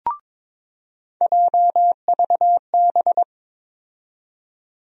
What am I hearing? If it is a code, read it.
Morse code, heard as a keyed tone, about 22 wpm, 700 Hz. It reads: JVB
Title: USA Call Sign Postfixes - 3 Letters 22wpm